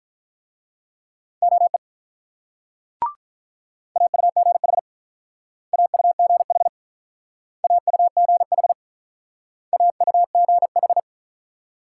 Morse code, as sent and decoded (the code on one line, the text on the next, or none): --- . / .- ..- --. .... / .- ..- --. .... / .- ..- --. .... / .- ..- --. ....
OE AUGH AUGH AUGH AUGH